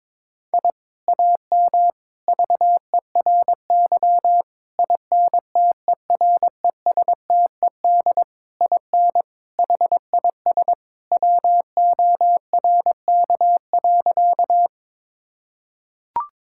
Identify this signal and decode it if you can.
Morse code, heard as a keyed tone, about 22 wpm, 700 Hz. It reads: I AM VERY INTERESTED IN HIS WORK.
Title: Sentences from Top 1000 Words - Review Part 6 - 22wpm